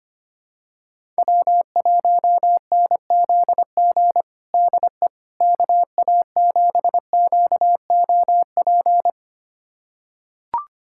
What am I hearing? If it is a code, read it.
Morse code, heard as a keyed tone, about 25 wpm, 700 Hz. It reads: W1NZG DE KA7QOP
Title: USA Callsign DE Callsign 25wpm